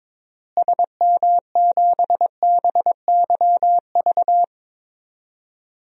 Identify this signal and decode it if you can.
Morse code, heard as a keyed tone, about 22 wpm, 700 Hz. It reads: SM7BYV
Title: DXCC-100 Call Signs - 6 Characters - Repeated 3x - 22wpm